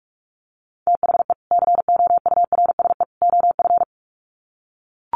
Morse code, heard as a keyed tone, about 44 wpm, 700 Hz. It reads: THE COURSE OF
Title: Sets of 3 Words 45wpm